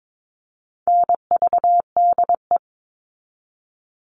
Morse code, heard as a keyed tone, about 22 wpm, 700 Hz. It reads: NVDE
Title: All Cut Numbers - 4-Digit 22wpm